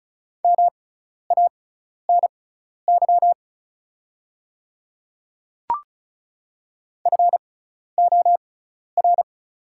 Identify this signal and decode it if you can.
Morse code, heard as a keyed tone, about 36 wpm, spaced as for 12 wpm, 690 Hz. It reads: MANY FOR
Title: 6x Character Spacing - Top 100 Words - 35wpm